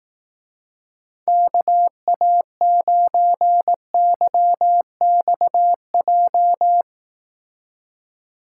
Morse code, heard as a keyed tone, about 18 wpm, 700 Hz. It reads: KA9YXJ
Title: Speed Racing - USA 2x3 Call Signs 15wpm